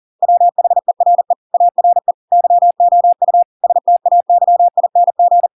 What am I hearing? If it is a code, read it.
Morse code, heard as a keyed tone, about 40 wpm, 700 Hz. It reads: WHERE ARE YOU STAYING